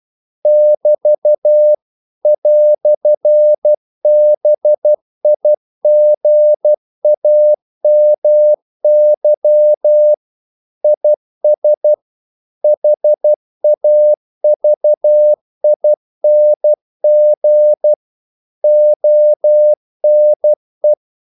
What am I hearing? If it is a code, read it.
Morse code, heard as a keyed tone, about 12 wpm, 600 Hz. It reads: <BT> "BIGAMY IS HAVING ONE
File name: CW / Practice / EnglishQuotations0312